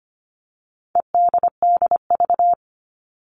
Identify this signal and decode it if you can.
Morse code, heard as a keyed tone, about 25 wpm, 700 Hz. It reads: EDDV
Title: All Cut Numbers - 4-Digit 25wpm